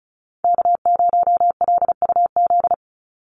Morse code, heard as a keyed tone, about 34 wpm, 700 Hz. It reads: K0LUZ
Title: Call Signs Repeated 2x - 35wpm